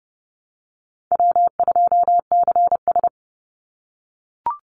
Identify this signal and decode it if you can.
Morse code, heard as a keyed tone, about 30 wpm, 700 Hz. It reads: W2CS
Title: Call Signs - 30wpm